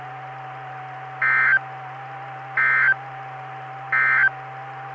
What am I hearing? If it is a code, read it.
Emergency Alert System (SAME) tones, heard as harsh buzzing bursts. It is heard from 1.1 to 4.4 s.
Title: Saline, AK SKYWARN Repeater System - W5RHS